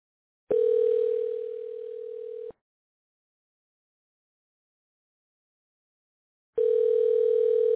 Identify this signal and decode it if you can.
Ringback tone (North America), 440 + 480 Hz.